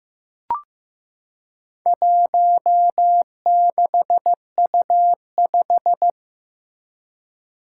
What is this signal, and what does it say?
Morse code, heard as a keyed tone, about 15 wpm, 700 Hz. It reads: 16U5
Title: Four Challenging Letter-Number Combinations 15wpm